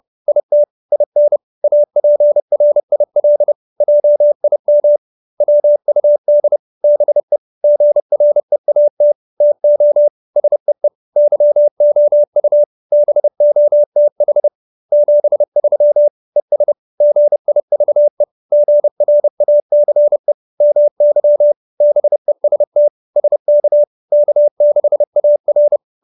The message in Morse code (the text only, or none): IT IN APRIL JIM WUD BE GREAT TO SEE YOU BOTH 73 ES GIVE GRACE MY BEST SK K6AR